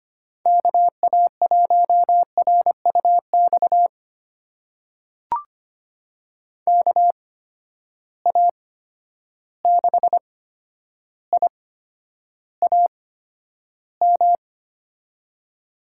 Morse code, heard as a keyed tone, about 25 wpm, spaced as for 11 wpm, 700 Hz. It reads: KA1RUX KA6IAM
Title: USA Call Signs - 6 Characters - 8x Character Spacing - 25wpm